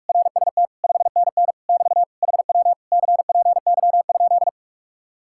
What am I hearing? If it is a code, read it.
Morse code, heard as a keyed tone, about 49 wpm, 700 Hz. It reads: RST 5NN <BT> HW CPY?